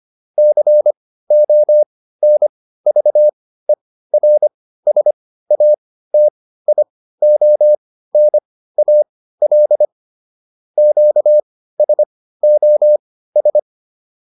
Morse code, heard as a keyed tone, about 25 wpm, spaced as for 15 wpm, 600 Hz. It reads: CONVERSATIONAL QSOS